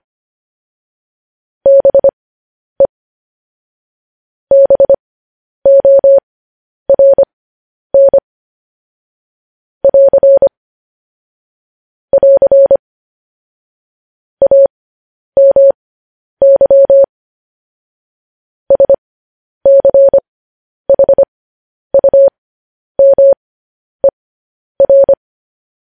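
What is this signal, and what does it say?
Morse code, heard as a keyed tone, about 25 wpm, spaced as for 10 wpm, 550 Hz. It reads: BE BORN <AR> <AR> AMY SCHUMER
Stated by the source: Morse code news headlines practice, at 10 WPM.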